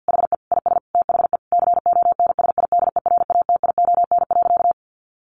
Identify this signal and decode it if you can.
Morse code, heard as a keyed tone, about 50 wpm, 700 Hz. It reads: HE IS THE CONSIDERATION.
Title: Sentences from Top 1500 Words 50wpm